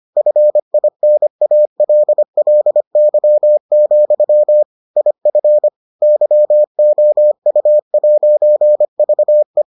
Morse code, heard as a keyed tone, about 25 wpm, 600 Hz. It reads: FINALLY, IF YOU'VE